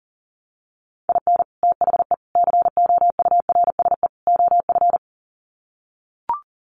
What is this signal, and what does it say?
Morse code, heard as a keyed tone, about 40 wpm, 700 Hz. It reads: IN THE COURSE OF
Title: Sets of 4 Words 40wpm